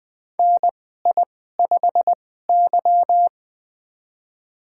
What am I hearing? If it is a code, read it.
Morse code, heard as a keyed tone, about 20 wpm, spaced as for 15 wpm, 700 Hz. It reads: NI5Y
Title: USA Call Signs - 4 Characters - 2x Character Spacing - 20wpm